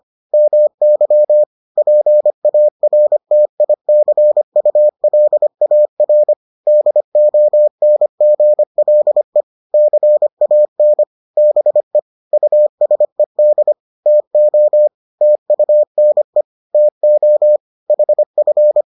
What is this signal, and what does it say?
Morse code, heard as a keyed tone, about 25 wpm, 600 Hz. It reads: MY PARTICULAR DONGLE CAN BE USED TO TUNE TO HF